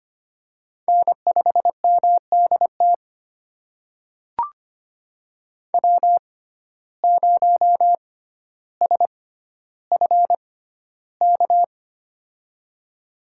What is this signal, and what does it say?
Morse code, heard as a keyed tone, about 25 wpm, spaced as for 13 wpm, 700 Hz. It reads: N5MDT W0SFK
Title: USA Call Signs - 6x Character Spacing - 25wpm